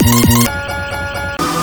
There is no code.